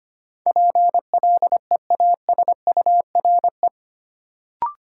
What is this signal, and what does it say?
Morse code, heard as a keyed tone, about 25 wpm, 700 Hz. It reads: PLEASURE
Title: Sentences from Top 1600 Words - New Word Drill Alt - 25wpm